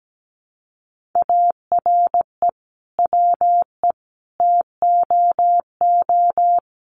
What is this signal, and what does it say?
Morse code, heard as a keyed tone, about 17 wpm, 700 Hz. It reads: ARE WE TOO
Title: Sentences from Top 300 Words 17wpm